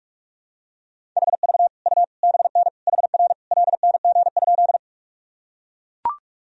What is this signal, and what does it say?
Morse code, heard as a keyed tone, about 41 wpm, 700 Hz. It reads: HV U BN HR LNG?
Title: Sentences from Top 200 Words and Abbreviations 45wpm